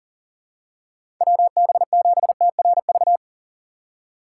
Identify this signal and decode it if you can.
Morse code, heard as a keyed tone, about 40 wpm, 700 Hz. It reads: WB7TRV
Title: USA Call Signs - 6 Characters - 4x Character Spacing - 40wpm